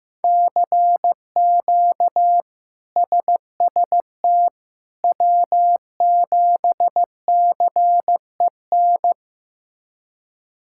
Morse code, heard as a keyed tone, about 16 wpm, 700 Hz. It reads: CQ SST W7CEN
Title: K1USN SST QSO Practice - 15wpm